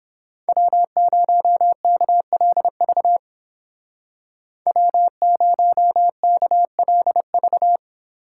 Morse code, heard as a keyed tone, about 27 wpm, 700 Hz. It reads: W0KLV W0KLV
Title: Speed Racing - USA 1x3 Call Signs 22wpm